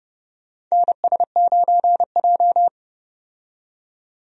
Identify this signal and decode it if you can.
Morse code, heard as a keyed tone, about 30 wpm, 700 Hz. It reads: NS9J
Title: USA Call Signs - 4 Characters - 30wpm